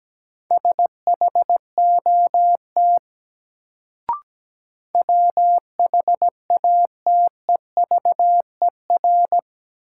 Morse code, heard as a keyed tone, about 17 wpm, 700 Hz. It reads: SHOT WHATEVER
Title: Sentences from Top 1100 Words - New Word Drill - 17wpm